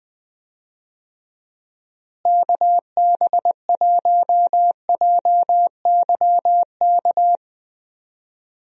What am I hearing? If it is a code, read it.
Morse code, heard as a keyed tone, about 20 wpm, 700 Hz. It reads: KB1JYK
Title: USA Call Signs - 6 Characters - Repeated 2x - 20wpm